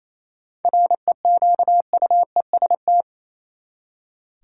Morse code, heard as a keyed tone, about 28 wpm, 700 Hz. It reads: REQUEST
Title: Seven Letter Words - 28wpm